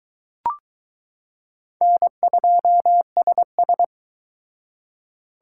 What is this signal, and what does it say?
Morse code, heard as a keyed tone, about 21 wpm, 700 Hz. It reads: N2SS